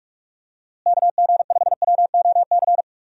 Morse code, heard as a keyed tone, about 41 wpm, 700 Hz. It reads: KG5WOC
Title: USA Call Signs - 6 Characters - Repeated 3x - 45wpm